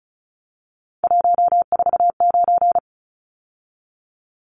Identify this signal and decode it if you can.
Morse code, heard as a keyed tone, about 34 wpm, 700 Hz. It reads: 149
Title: Numbers - 3-Digit 35wpm